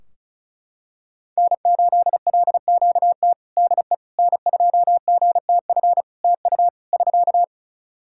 Morse code, heard as a keyed tone, about 36 wpm, 700 Hz. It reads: N8LQT DE N2GTF TU <SK>
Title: K1USN SST QSO Practice 35wpm